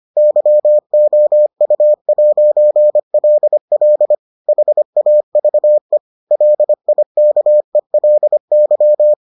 Morse code, heard as a keyed tone, about 25 wpm, 600 Hz. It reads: YOU'LL HAVE LIKELY